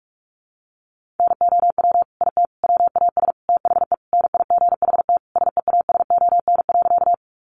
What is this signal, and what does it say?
Morse code, heard as a keyed tone, about 44 wpm, 700 Hz. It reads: NOW IT WAS THE NIGHT SEASON.
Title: Sentences from Top 1400 Words 45wpm